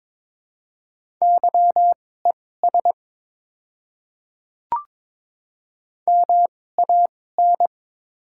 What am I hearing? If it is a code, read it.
Morse code, heard as a keyed tone, about 22 wpm, spaced as for 16 wpm, 700 Hz. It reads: YES MAN
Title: 2x Character Spacing - Top 500 Words - 22wpm